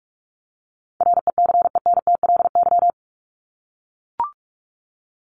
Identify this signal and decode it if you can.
Morse code, heard as a keyed tone, about 44 wpm, 700 Hz. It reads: RECENTLY
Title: LY Suffix - Top 100 Words with Base Words 45wpm